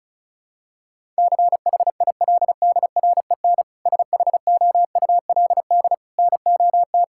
Morse code, heard as a keyed tone, about 34 wpm, 700 Hz. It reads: CHILDREN SHOULD NOT
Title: Sentences from Top 1000 Words - Review Part 5 - 35wpm